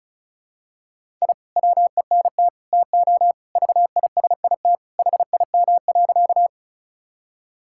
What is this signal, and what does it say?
Morse code, heard as a keyed tone, about 36 wpm, 700 Hz. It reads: I WENT TO VISIT HIM.
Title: Sentences from Top 1000 Words - Review Part 6 - 35wpm